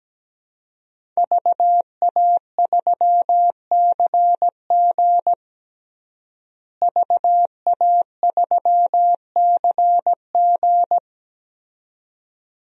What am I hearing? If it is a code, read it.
Morse code, heard as a keyed tone, about 17 wpm, 700 Hz. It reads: VA3CG VA3CG
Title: DXCC-100 Call Signs - 5 Characters - Repeated 2x - 17wpm